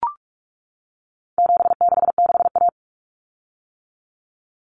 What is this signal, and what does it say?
Morse code, heard as a keyed tone, about 44 wpm, 700 Hz. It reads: 766A
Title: Common Cut Numbers - 4-Digit 45wpm